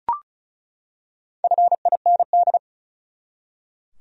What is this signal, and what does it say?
Morse code, heard as a keyed tone, about 36 wpm, 700 Hz. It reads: FIND
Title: Top 100 Words 35wpm